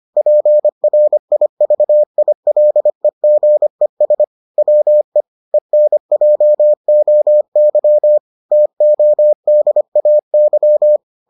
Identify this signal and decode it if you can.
Morse code, heard as a keyed tone, about 25 wpm, 600 Hz. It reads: PRIVILEGES WE ENJOY TODAY